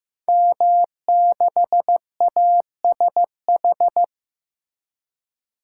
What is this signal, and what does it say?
Morse code, heard as a keyed tone, about 15 wpm, 700 Hz. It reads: M6ASH
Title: Code Groups - 5 Letter:Numbers - 15wpm